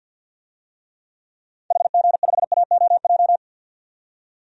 Morse code, heard as a keyed tone, about 52 wpm, 700 Hz. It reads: HG5AOJ
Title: DXCC-100 Call Signs - 6 Characters - 50wpm